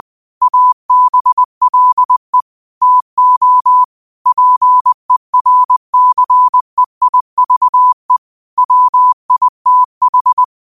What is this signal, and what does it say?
Morse code, heard as a keyed tone, about 20 wpm, 1000 Hz. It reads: ABLE TO PERCEIVE WITH
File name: qotdfiles / 20wpm